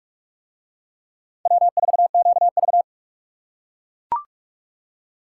Morse code, heard as a keyed tone, about 42 wpm, 700 Hz. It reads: W4QV